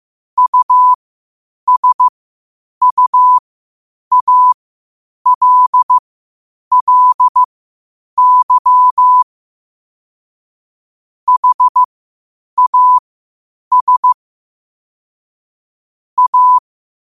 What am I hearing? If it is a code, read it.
Morse code, heard as a keyed tone, about 15 wpm, spaced as for 8 wpm, 1000 Hz. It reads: USUALLY HAS A
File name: qotdfiles / 8wpm